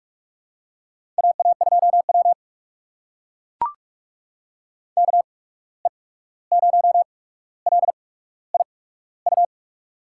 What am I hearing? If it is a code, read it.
Morse code, heard as a keyed tone, about 44 wpm, spaced as for 17 wpm, 700 Hz. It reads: AA2W KE0LIU
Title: USA Call Signs - 8x Character Spacing - 45wpm